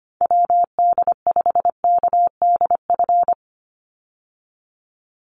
Morse code, heard as a keyed tone, about 25 wpm, 700 Hz. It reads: WD5KDF
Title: USA Call Signs 25wpm